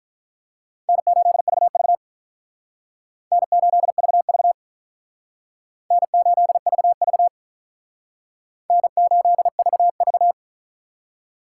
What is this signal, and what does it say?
Morse code, heard as a keyed tone, about 45 wpm, 700 Hz. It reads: N8VV N8VV N8VV N8VV